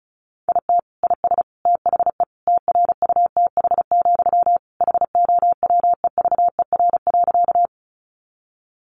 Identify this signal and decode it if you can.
Morse code, heard as a keyed tone, about 34 wpm, 700 Hz. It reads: IT IS THE TRUTH, HOWEVER.
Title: Sentences from Top 800 Words 35wpm